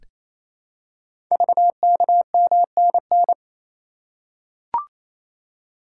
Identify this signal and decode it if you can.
Morse code, heard as a keyed tone, about 28 wpm, 690 Hz. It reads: VKMNN
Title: Code Groups - 5 Letters - 28wpm